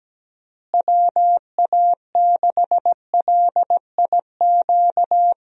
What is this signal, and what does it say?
Morse code, heard as a keyed tone, about 17 wpm, 700 Hz. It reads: WA6LIQ
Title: USA Call Signs - 6 Characters - Repeated 2x - 17wpm